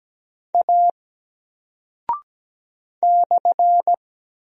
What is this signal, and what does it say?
Morse code, heard as a keyed tone, about 17 wpm, 700 Hz. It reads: A /
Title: Single Character (Letter:Number:Punctuation:Prosign) Alt 17wpm